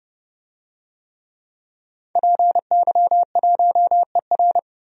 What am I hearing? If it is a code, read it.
Morse code, heard as a keyed tone, about 32 wpm, 700 Hz. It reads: PY1ER
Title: DXCC-100 Call Signs - 5 Characters - 30wpm